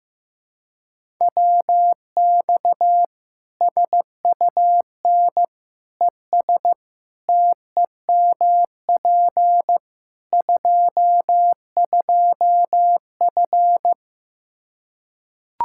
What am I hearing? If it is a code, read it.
Morse code, heard as a keyed tone, about 15 wpm, 700 Hz. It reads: WX SUN ES TEMP 22F
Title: Practice QSO - Exchange 2 - Info - 15wpm